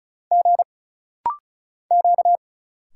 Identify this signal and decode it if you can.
Morse code, heard as a keyed tone, about 36 wpm, 700 Hz. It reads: G Q
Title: Single Letters - Rapid-Fire 35wpm